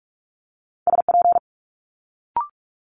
Morse code, heard as a keyed tone, about 46 wpm, 700 Hz. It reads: SP
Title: Multi-letter Phonemes Quick Fire 45wpm